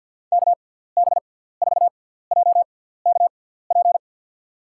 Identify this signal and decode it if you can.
Morse code, heard as a keyed tone, about 48 wpm, spaced as for 17 wpm, 700 Hz. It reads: KB4JKP
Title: USA Call Signs - 6 Characters - 6x Character Spacing - 50wpm